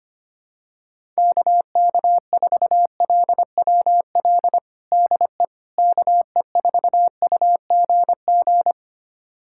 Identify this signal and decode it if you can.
Morse code, heard as a keyed tone, about 25 wpm, 700 Hz. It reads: KK4LWL DE KE4UGG
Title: USA Callsign DE Callsign - 6 Characters - 25wpm